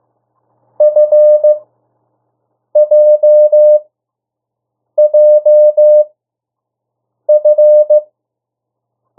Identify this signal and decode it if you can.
Morse code, heard as a keyed tone, about 15 wpm, spaced as for 6 wpm, 600 Hz. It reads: FJJF
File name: morse mp3 / 02 urok 1